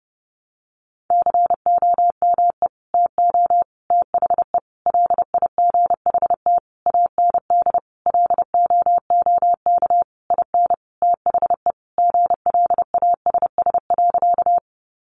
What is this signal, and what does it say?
Morse code, heard as a keyed tone, about 30 wpm, 700 Hz. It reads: COME TO THE LIGHT AND LOOK IN THE GLASS.
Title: Sentences from Top 1300 Words 30wpm